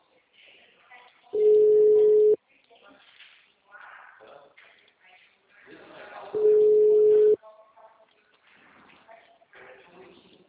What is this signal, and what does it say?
Ringback tone (Europe, 425 Hz), with speech.